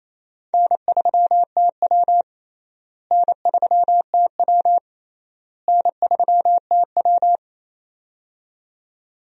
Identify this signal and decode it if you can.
Morse code, heard as a keyed tone, about 28 wpm, 700 Hz. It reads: N3TW N3TW N3TW
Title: USA Call Signs - 4 Characters - Repeated 3x - 28wpm